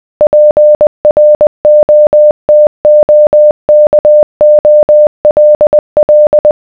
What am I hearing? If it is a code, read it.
Morse code, heard as a keyed tone, about 20 wpm, 600 Hz. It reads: PROTOKOLL